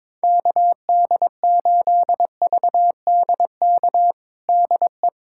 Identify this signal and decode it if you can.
Morse code, heard as a keyed tone, about 22 wpm, 700 Hz. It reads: KD8VDK DE
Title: USA Callsign DE Callsign 22wpm